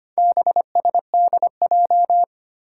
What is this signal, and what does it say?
Morse code, heard as a keyed tone, about 25 wpm, 700 Hz. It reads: BSDJ
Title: Four Challenging Letter-Number Combinations 25wpm